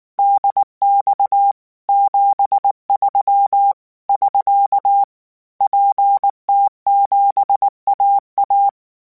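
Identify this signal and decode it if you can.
Morse code, heard as a keyed tone, about 19 wpm, 800 Hz. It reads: DX 73 <SK> PT7AA
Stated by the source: Morse code practice text at 18 wpm, from Portal do CW.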